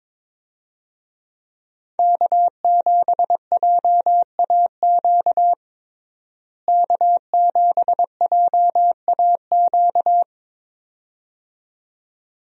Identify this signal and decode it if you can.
Morse code, heard as a keyed tone, about 22 wpm, 700 Hz. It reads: K7JAQ K7JAQ
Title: USA Call Signs - 5 Characters - Repeated 2x - 22wpm